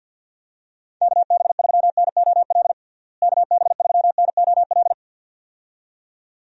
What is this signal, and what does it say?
Morse code, heard as a keyed tone, about 51 wpm, 700 Hz. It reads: KB3NOL KB3NOL
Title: USA Call Signs - 6 Characters - Repeated 2x - 50wpm